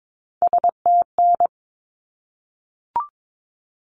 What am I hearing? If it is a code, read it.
Morse code, heard as a keyed tone, about 22 wpm, 700 Hz. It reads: STN
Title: Abbreviations 22wpm